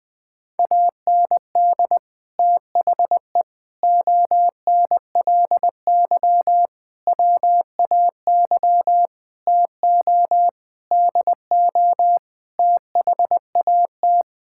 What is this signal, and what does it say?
Morse code, heard as a keyed tone, about 20 wpm, 700 Hz. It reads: AND THE ONLY WAY TO DO THAT
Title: Sets of 9 Words - 20wpm